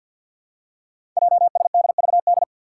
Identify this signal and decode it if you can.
Morse code, heard as a keyed tone, about 52 wpm, 700 Hz. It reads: JSDVD
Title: Code Groups - 5 Letters - 50wpm